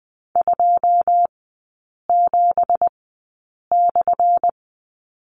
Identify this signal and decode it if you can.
Morse code, heard as a keyed tone, about 20 wpm, spaced as for 8 wpm, 700 Hz. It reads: 27/
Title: 178 Lesson 29e - Slash - Three Characters 20wpm